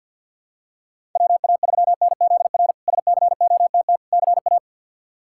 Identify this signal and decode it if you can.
Morse code, heard as a keyed tone, about 46 wpm, 700 Hz. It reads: WA3NZR SCOTT CA
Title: K1USN SST QSO Practice - 50wpm